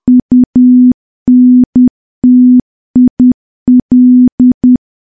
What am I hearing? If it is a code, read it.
Morse code, heard as a keyed tone, about 10 wpm, 250 Hz. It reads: UNTIL
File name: morse / morse-training-10WPM